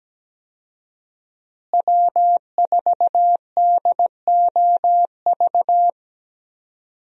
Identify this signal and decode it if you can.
Morse code, heard as a keyed tone, about 17 wpm, 700 Hz. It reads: W4DOV
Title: USA Call Signs - 5 Characters - Repeated 3x - 17wpm